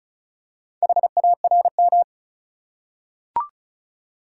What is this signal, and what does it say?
Morse code, heard as a keyed tone, about 35 wpm, 700 Hz. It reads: HARM
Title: Four Letter Words - 35wpm